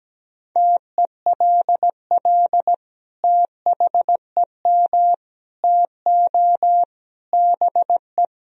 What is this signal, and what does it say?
Morse code, heard as a keyed tone, about 17 wpm, 700 Hz. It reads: TELL THEM TO BE
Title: Sentences from Top 1000 Words - Review Part 3 - 17wpm